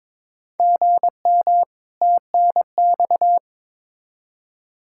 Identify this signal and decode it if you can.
Morse code, heard as a keyed tone, about 22 wpm, 700 Hz. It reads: GM TNX
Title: Practice QSO - Exchange 1 - Beginning - 22wpm